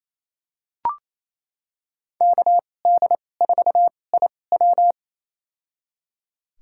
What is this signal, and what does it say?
Morse code, heard as a keyed tone, about 28 wpm, spaced as for 20 wpm, 700 Hz. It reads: KD4IW
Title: USA Call Signs - 5 Characters - 2x Character Spacing - 28wpm